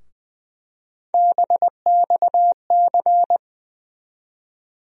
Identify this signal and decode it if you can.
Morse code, heard as a keyed tone, about 20 wpm, 700 Hz. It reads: BXC
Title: USA Call Sign Postfixes - 3 Letters 20wpm